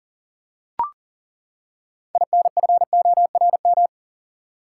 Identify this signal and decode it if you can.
Morse code, heard as a keyed tone, about 40 wpm, 700 Hz. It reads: INFORM
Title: Six Letter Words - 40wpm